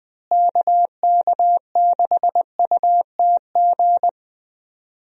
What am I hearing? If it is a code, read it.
Morse code, heard as a keyed tone, about 20 wpm, 700 Hz. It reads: KK6UTG
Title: USA Call Signs - 6 Characters - Repeated 3x - 20wpm